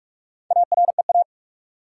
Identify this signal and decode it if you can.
Morse code, heard as a keyed tone, about 47 wpm, 700 Hz. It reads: AREA
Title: Four Letter Words - 45wpm